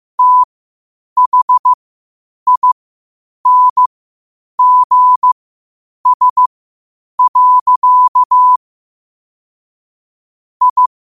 Morse code, heard as a keyed tone, about 15 wpm, spaced as for 8 wpm, 1000 Hz. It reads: THINGS. I
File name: qotdfiles / 8wpm